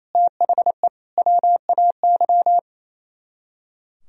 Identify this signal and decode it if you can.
Morse code, heard as a keyed tone, about 34 wpm, 700 Hz. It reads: THE WAY
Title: Speed-Racing - Sets of 3 Words 28wpm